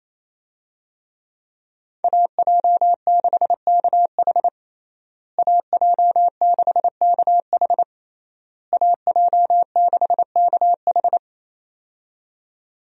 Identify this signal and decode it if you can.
Morse code, heard as a keyed tone, about 28 wpm, 700 Hz. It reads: AJ6KH AJ6KH AJ6KH
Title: USA Call Signs - 5 Characters - Repeated 3x - 28wpm